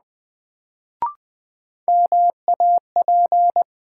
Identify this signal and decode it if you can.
Morse code, heard as a keyed tone, about 21 wpm, 700 Hz. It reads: MAP
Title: Sentences from Top 400 Words - New Word Drill - 20wpm